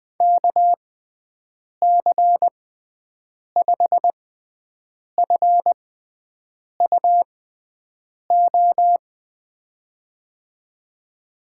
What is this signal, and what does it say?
Morse code, heard as a keyed tone, about 20 wpm, spaced as for 7 wpm, 700 Hz. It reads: KC5FUO
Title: USA Call Signs - 6 Characters - 6x Character Spacing - 20wpm